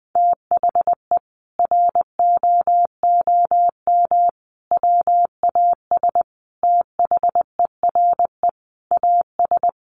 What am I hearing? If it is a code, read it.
Morse code, heard as a keyed tone, about 20 wpm, 700 Hz. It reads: THE ROOM WAS THERE AS
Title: A Sound Of Thunder 20wpm